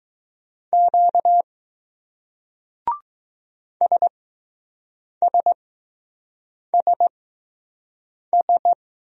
Morse code, heard as a keyed tone, about 20 wpm, spaced as for 6 wpm, 700 Hz. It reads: Q SSSS